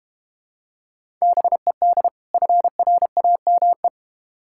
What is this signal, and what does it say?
Morse code, heard as a keyed tone, about 32 wpm, 700 Hz. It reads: BED FRAME